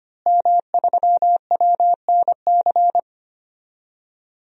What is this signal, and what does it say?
Morse code, heard as a keyed tone, about 25 wpm, 700 Hz. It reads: M3WNC
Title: DXCC-100 Call Signs - 5 Characters - Repeated 2x - 25wpm